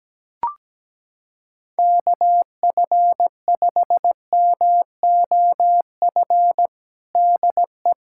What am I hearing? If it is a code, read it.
Morse code, heard as a keyed tone, about 17 wpm, 700 Hz. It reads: KF5MOF DE
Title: USA Callsign DE Callsign - 6 Characters - 17wpm